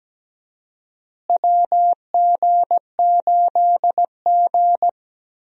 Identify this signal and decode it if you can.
Morse code, heard as a keyed tone, about 17 wpm, 700 Hz. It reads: WG8G
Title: USA Call Signs - 4 Characters - Repeated 2x - 17wpm